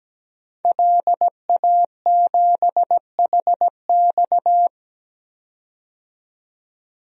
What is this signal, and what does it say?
Morse code, heard as a keyed tone, about 17 wpm, 700 Hz. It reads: LA7HX
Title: DXCC-100 Call Signs - 5 Characters - 17wpm